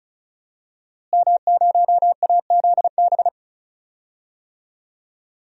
Morse code, heard as a keyed tone, about 36 wpm, 700 Hz. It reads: M0AZB
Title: DXCC-100 Call Signs - 5 Characters - Repeated 2x - 35wpm